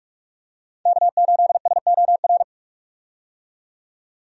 Morse code, heard as a keyed tone, about 47 wpm, 700 Hz. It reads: K8SOR
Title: USA Call Signs - 5 Characters - 45wpm